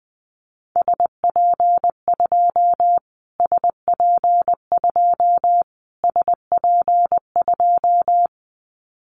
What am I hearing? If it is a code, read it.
Morse code, heard as a keyed tone, about 20 wpm, 700 Hz. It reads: SP2 SP2 SP2
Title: Callsign Prefixes for Top 25 DXCC - Repeated 3x - 20wpm